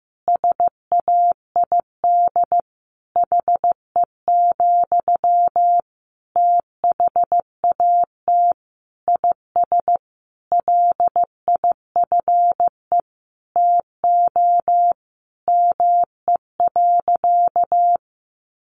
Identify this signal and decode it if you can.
Morse code, heard as a keyed tone, about 15 wpm, 700 Hz. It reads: SAID HE, THAT IS LIFE TO ME.
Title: Sentences from Top 500 Words 15wpm